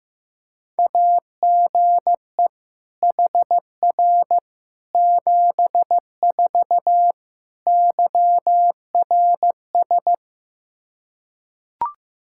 Morse code, heard as a keyed tone, about 15 wpm, 700 Hz. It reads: AGE HR 74 YRS
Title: Practice QSO - Exchange 3 - More Info - 15wpm